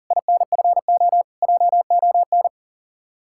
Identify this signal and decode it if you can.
Morse code, heard as a keyed tone, about 47 wpm, 700 Hz. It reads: INFO JON
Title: Speed-Racing Common QSO Phrases Encore - 40wpm